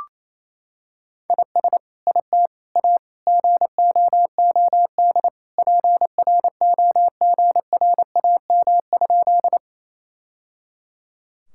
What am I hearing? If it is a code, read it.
Morse code, heard as a keyed tone, about 28 wpm, 700 Hz. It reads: IS IT A GOOD PROGRAM?
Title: Sentences from Top 1000 Words - Review Part 5 - 28wpm